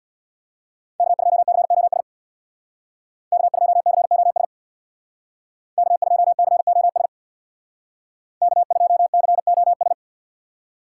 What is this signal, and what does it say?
Morse code, heard as a keyed tone, about 61 wpm, 700 Hz. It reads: K2COS K2COS K2COS K2COS
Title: Speed Racing - USA 1x3 Call Signs 50wpm